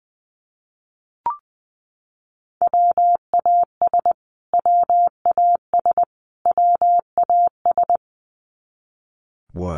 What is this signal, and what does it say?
Morse code, heard as a keyed tone, about 20 wpm, 700 Hz. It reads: WAS WAS WAS
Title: Top 200 Words Repeated 20wpm